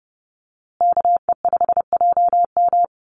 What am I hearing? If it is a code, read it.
Morse code, heard as a keyed tone, about 30 wpm, 700 Hz. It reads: KE5JM
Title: USA Call Signs Repeated 2x 30wpm